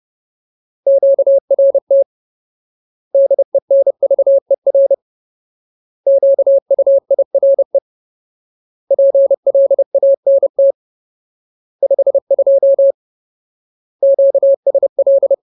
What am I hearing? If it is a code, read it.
Morse code, heard as a keyed tone, about 30 wpm, 550 Hz. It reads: QRT DENVER QUIRE PLANT 52 QSL
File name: Ultimate Word List / 30WPM Ultimate